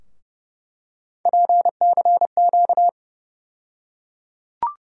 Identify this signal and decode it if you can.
Morse code, heard as a keyed tone, about 31 wpm, 700 Hz. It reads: PCQ
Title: USA Call Sign Postfixes - 3 Letters 30wpm